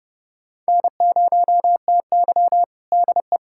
Morse code, heard as a keyed tone, about 32 wpm, 700 Hz. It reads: N0TY DE
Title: USA Callsign DE Callsign - 4 Characters - 30wpm